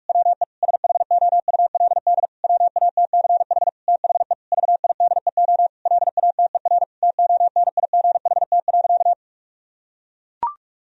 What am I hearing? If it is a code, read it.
Morse code, heard as a keyed tone, about 41 wpm, 700 Hz. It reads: WE SHOULD WATCH THE VIDEO LATER TONIGHT.
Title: Sentences from Top 1000 Words - Review Part 4 - 45wpm